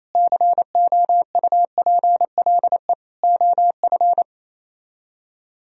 Morse code, heard as a keyed tone, about 35 wpm, 700 Hz. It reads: COUPLE OF
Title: Speed-Racing - Sets of 3 Words 28wpm